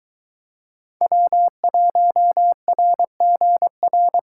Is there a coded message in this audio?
Yes: Morse code from 1.0 to 4.2 s, heard as a keyed tone, about 22 wpm, 700 Hz. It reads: W1RGR